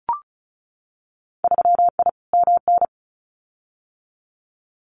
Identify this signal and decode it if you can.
Morse code, heard as a keyed tone, about 34 wpm, 700 Hz. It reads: 3I MN
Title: Winter Field Day Exchanges 35wpm